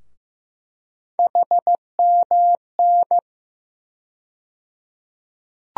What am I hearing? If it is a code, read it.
Morse code, heard as a keyed tone, about 15 wpm, 700 Hz. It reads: HMN
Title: USA Call Sign Postfixes - 3 Letters 15wpm